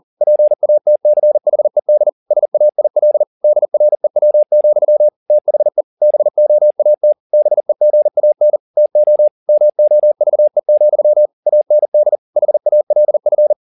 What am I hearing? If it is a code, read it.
Morse code, heard as a keyed tone, about 40 wpm, 600 Hz. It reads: PATCHED SAIL DREW, THE BOAT BEGAN TO MOVE, AND HALF
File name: the old man and the sea 2 40wpm